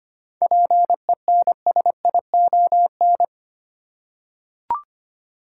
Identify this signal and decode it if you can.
Morse code, heard as a keyed tone, about 25 wpm, 700 Hz. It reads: PENSION
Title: Seven Letter Words - 25wpm